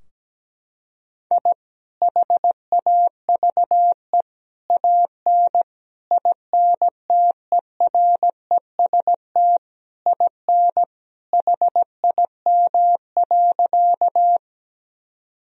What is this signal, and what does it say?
Morse code, heard as a keyed tone, about 17 wpm, 700 Hz. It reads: I HAVE AN INTEREST IN HIM.
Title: Sentences from Top 1000 Words - Review Part 2 - 17wpm